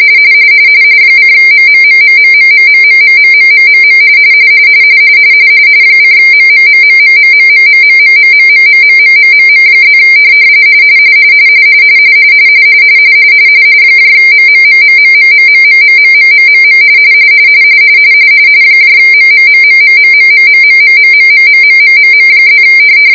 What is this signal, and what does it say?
RTTY, about 45.45 baud. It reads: :MMMMMMMMMMMMMMM :MMMMMMMMMMMMMMMMMMMM+ MMMMMMMMMMMMMMM MMMMMMMMMMMMMMMMMMM+